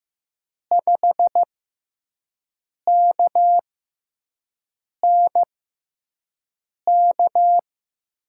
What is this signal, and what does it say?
Morse code, heard as a keyed tone, about 15 wpm, spaced as for 5 wpm, 700 Hz. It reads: 5KNK
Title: USA Call Signs - 6 Characters - 6x Character Spacing - 15wpm